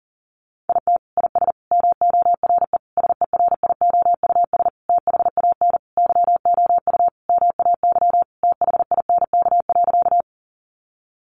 Morse code, heard as a keyed tone, about 40 wpm, 700 Hz. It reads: IT IS MORE SERIOUS THAN YOU MAY THINK.
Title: Sentences from Top 1100 Words 40wpm